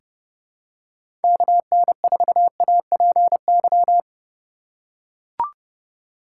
Morse code, heard as a keyed tone, about 30 wpm, 700 Hz. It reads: KN4APY
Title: USA Call Signs - 6 Characters - 3x Character Spacing - 30wpm